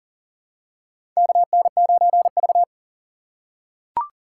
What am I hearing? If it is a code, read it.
Morse code, heard as a keyed tone, about 37 wpm, 700 Hz. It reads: KN9V
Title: USA Call Signs - 4 Characters - 3x Character Spacing - 40wpm